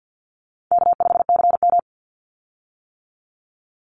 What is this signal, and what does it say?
Morse code, heard as a keyed tone, about 50 wpm, 700 Hz. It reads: K5CM
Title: Easy Call Signs - 50wpm